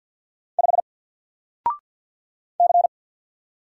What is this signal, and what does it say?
Morse code, heard as a keyed tone, about 50 wpm, 700 Hz. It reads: 5 C
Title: Single Character (Letter:Number:Punctuation:Prosign) 50wpm